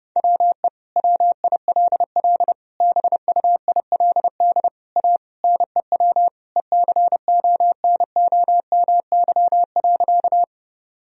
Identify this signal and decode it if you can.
Morse code, heard as a keyed tone, about 30 wpm, 700 Hz. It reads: WE WILL BUILD A NEW ECONOMY.
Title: Sentences from Top 1000 Words - Review Part 3 - 30wpm